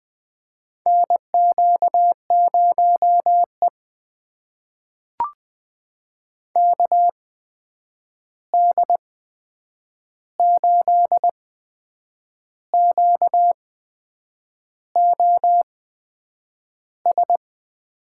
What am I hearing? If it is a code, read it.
Morse code, heard as a keyed tone, about 20 wpm, spaced as for 7 wpm, 700 Hz. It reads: NQ0E KD8QOS
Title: USA Call Signs - 8x Character Spacing - 20wpm